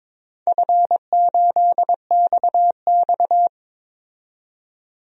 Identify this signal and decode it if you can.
Morse code, heard as a keyed tone, about 21 wpm, 700 Hz. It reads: F8XX
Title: DXCC-100 Call Signs - 4 Characters - 22wpm